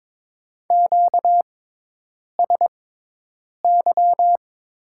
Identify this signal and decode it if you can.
Morse code, heard as a keyed tone, about 22 wpm, spaced as for 8 wpm, 700 Hz. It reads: QSY
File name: Top 90 QSO Elements - 6x Character Spacing - 22wpm